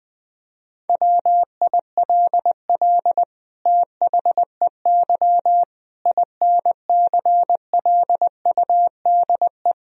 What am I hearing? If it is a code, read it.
Morse code, heard as a keyed tone, about 21 wpm, 700 Hz. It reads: WILL THEY INCLUDE
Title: Sentences from Top 1000 Words - Review Part 3 - 20wpm